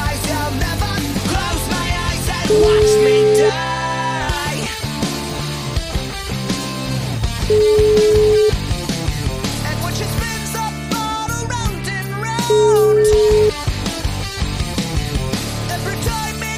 Ringback tone (Europe, 425 Hz), with noise.